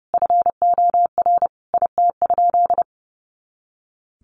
Morse code, heard as a keyed tone, about 28 wpm, 700 Hz. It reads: FOR IT?
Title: Sentences from Top 400 Words 30wpm